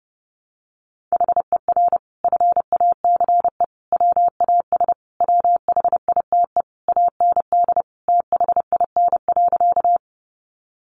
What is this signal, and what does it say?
Morse code, heard as a keyed tone, about 28 wpm, 700 Hz. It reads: HER FACE WAS WHITE AND THIN.
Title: Sentences from Top 1200 Words 30wpm